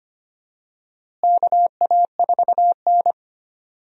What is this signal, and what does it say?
Morse code, heard as a keyed tone, about 25 wpm, 700 Hz. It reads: KA4N
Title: USA Call Signs - 4 Characters - Repeated 2x - 25wpm